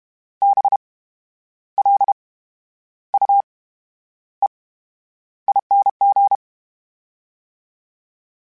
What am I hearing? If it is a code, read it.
Morse code, heard as a keyed tone, about 32 wpm, spaced as for 11 wpm, 800 Hz. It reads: BLUEING
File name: Special INGs / Special INGs-mp3s / VerySlowToFaster Ing @30